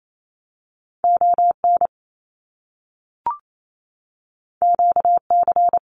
Morse code, heard as a keyed tone, about 28 wpm, 700 Hz. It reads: ON QC
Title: Canadian Provinces and Territories 28wpm